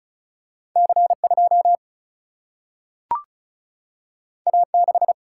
Morse code, heard as a keyed tone, about 36 wpm, 700 Hz. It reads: C2 A6
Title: Single-Letter Single-Number 35wpm